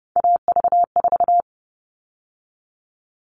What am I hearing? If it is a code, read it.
Morse code, heard as a keyed tone, about 30 wpm, 700 Hz. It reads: AV4
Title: All Cut Numbers - 3-Digit 30wpm